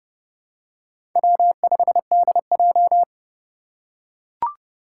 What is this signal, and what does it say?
Morse code, heard as a keyed tone, about 30 wpm, 700 Hz. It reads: W5DJ
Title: Four Challenging Letter-Number Combinations 30wpm